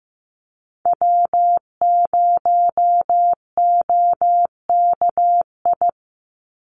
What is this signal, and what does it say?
Morse code, heard as a keyed tone, about 15 wpm, 700 Hz. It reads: W0OKI
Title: USA Call Signs 15wpm